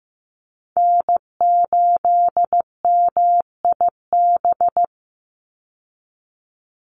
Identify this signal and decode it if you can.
Morse code, heard as a keyed tone, about 15 wpm, 700 Hz. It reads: N8MIB
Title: USA Call Signs 15wpm